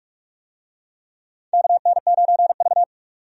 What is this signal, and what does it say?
Morse code, heard as a keyed tone, about 43 wpm, 700 Hz. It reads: KN9V
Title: USA Call Signs - 4 Characters - 45wpm